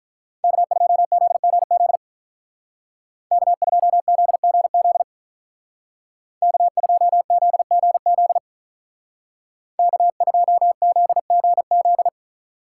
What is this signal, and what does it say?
Morse code, heard as a keyed tone, about 47 wpm, 700 Hz. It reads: K2ZGZ K2ZGZ K2ZGZ K2ZGZ